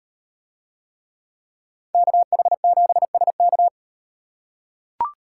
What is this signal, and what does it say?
Morse code, heard as a keyed tone, about 38 wpm, 700 Hz. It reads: KH7SK